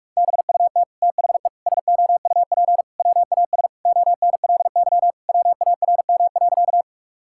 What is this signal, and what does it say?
Morse code, heard as a keyed tone, about 46 wpm, 700 Hz. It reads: BUT THE SOUP WAS ONLY WARM.
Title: Sentences from Top 1000 Words - Review Part 6 - 45wpm